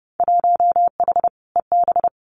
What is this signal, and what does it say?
Morse code, heard as a keyed tone, about 30 wpm, 700 Hz. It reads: 1H EB
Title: Winter Field Day Exchanges 30wpm